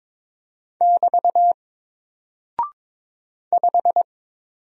Morse code, heard as a keyed tone, about 22 wpm, 700 Hz. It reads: <BT> 5
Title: Single Character (Letter:Number:Punctuation:Prosign) Alt 22wpm